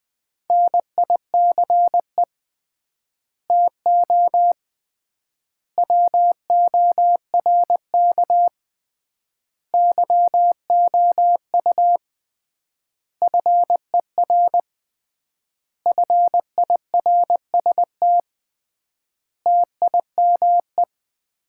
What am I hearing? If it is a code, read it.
Morse code, heard as a keyed tone, about 20 wpm, 700 Hz. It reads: NICE TO WORK YOU FER FIRST TIME
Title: Common QSO Phrases - 3x Word-Spacing - 20wpm